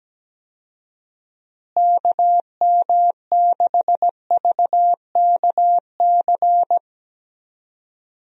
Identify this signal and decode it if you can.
Morse code, heard as a keyed tone, about 17 wpm, 700 Hz. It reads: KM6VKC
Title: USA Call Signs - 6 Characters - Repeated 2x - 17wpm